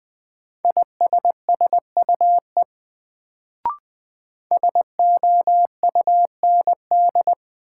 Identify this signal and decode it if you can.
Morse code, heard as a keyed tone, about 20 wpm, 700 Hz. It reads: ISSUE SOUND
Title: Sentences from Top 900 Words - New Word Drill - 20wpm